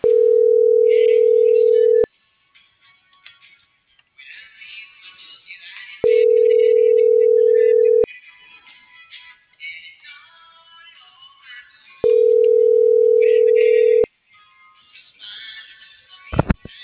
Ringback tone (North America), 440 + 480 Hz.